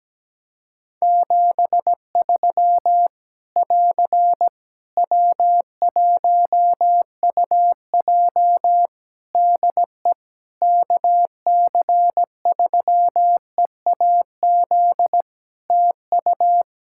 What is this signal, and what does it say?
Morse code, heard as a keyed tone, about 17 wpm, 700 Hz. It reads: 73 <AR> W1UJ DE KC3EAZ TU
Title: Practice QSO - Exchange 4 - Ending - 17wpm